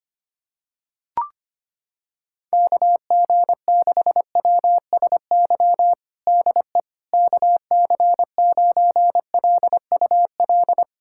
Morse code, heard as a keyed tone, about 25 wpm, 700 Hz. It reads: KG6WSY DE KC9LUL
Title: USA Callsign DE Callsign - 6 Characters - 25wpm